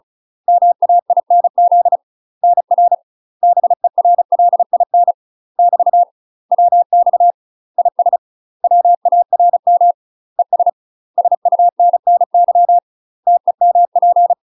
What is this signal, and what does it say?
Morse code, heard as a keyed tone, about 36 wpm, 700 Hz. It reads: MAINZ NR BERLIN <BT> WX IS WARM ES SUNNY TEMP
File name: UK MASTER QSOs TRAINING / 35WPM / UK MASTER QSOs 0005